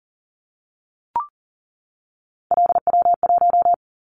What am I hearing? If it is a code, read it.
Morse code, heard as a keyed tone, about 40 wpm, 700 Hz. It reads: LW1
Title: Callsign Prefixes for Top 50 DXCC - 40wpm